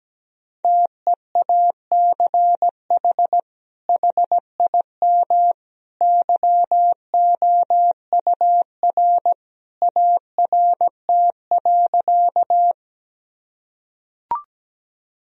Morse code, heard as a keyed tone, about 17 wpm, 700 Hz. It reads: TEACH HIM YOUR ART.
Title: Sentences from Top 1000 Words - Review Part 3 - 17wpm